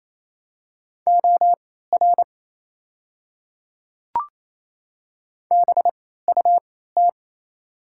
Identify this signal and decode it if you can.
Morse code, heard as a keyed tone, about 28 wpm, spaced as for 16 wpm, 700 Hz. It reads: OR BUT
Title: 3x Character Spacing - Top 100 Words - 28wpm